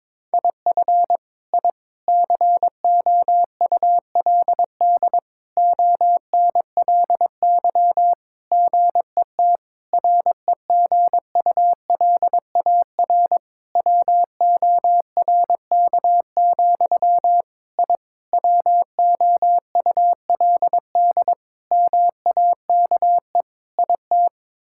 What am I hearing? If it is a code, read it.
Morse code, heard as a keyed tone, about 22 wpm, 700 Hz. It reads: IF I COULD ONLY GET REGULAR WORK, I WOULD MAKE IT
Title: Sentences from Top 1000 Words - Review Part 6 - 22wpm